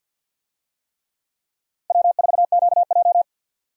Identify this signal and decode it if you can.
Morse code, heard as a keyed tone, about 47 wpm, 700 Hz. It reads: W4QJ
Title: USA Call Signs - 4 Characters - 50wpm